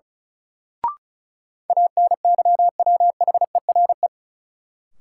Morse code, heard as a keyed tone, about 36 wpm, 700 Hz. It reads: ANYWHERE
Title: Sentences from Top 1400 Words - New Word Drill Alt - 35wpm